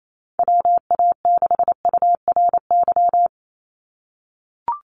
Morse code, heard as a keyed tone, about 28 wpm, 700 Hz. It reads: WA6URY
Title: Call Signs - 28wpm